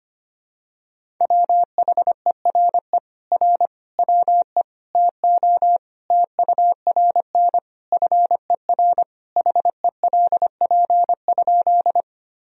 Morse code, heard as a keyed tone, about 25 wpm, 700 Hz. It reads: WHERE R WE TO TURN FER HELP?
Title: Sentences from Top 200 Words and Abbreviations 25wpm